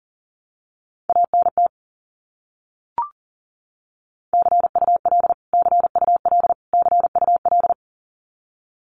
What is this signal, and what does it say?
Morse code, heard as a keyed tone, about 40 wpm, 700 Hz. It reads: ANT CUL CUL CUL
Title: Abbreviations Repeated 3x - 40wpm